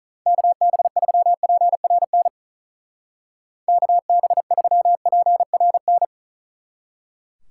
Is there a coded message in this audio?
Yes: Morse code from 0.3 to 6.1 s, heard as a keyed tone, about 44 wpm, 700 Hz. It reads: KB3PRN KB3PRN